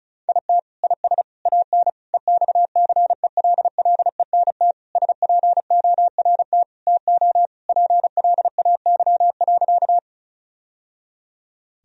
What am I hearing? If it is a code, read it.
Morse code, heard as a keyed tone, about 36 wpm, 700 Hz. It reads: IT IS AN EXCELLENT SPORT TO PLAY.
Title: Sentences from Top 1000 Words - Review Part 4 - 35wpm